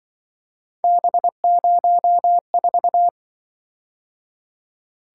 Morse code, heard as a keyed tone, about 25 wpm, 700 Hz. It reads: B04
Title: Speed-Racing - Three Letter-Number Combinations - 20wpm